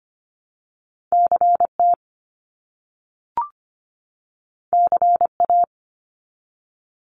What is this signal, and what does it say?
Morse code, heard as a keyed tone, about 25 wpm, 700 Hz. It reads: CT CA
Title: US States 25wpm